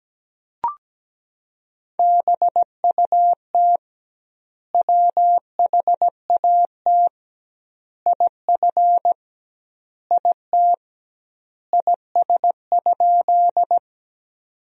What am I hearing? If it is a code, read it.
Morse code, heard as a keyed tone, about 17 wpm, 700 Hz. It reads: BUT WHAT IF IT IS?
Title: 2x Word-Spacing - Sentences from Top 100 Words - 17wpm